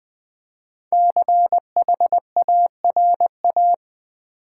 Morse code, heard as a keyed tone, about 21 wpm, 700 Hz. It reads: CHARA
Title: Build A Word - 5 Letters At A Time - 12-Letter Words - 20wpm